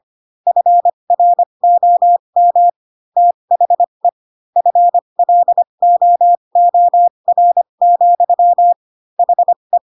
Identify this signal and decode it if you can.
Morse code, heard as a keyed tone, about 25 wpm, spaced as for 22 wpm, 700 Hz. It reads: FROM THE FLOOR, HE
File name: Pinocchio / Pinocchio ch7 18 25 ik1hge